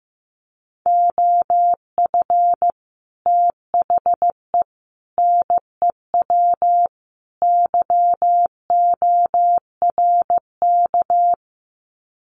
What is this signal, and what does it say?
Morse code, heard as a keyed tone, about 15 wpm, 700 Hz. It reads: OF THE NEW YORK
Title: Sets of 4 Words 15wpm